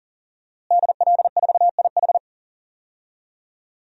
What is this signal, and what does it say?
Morse code, heard as a keyed tone, about 40 wpm, 700 Hz. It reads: DL4IH
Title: DXCC-100 Call Signs - 5 Characters - Repeated 3x - 40wpm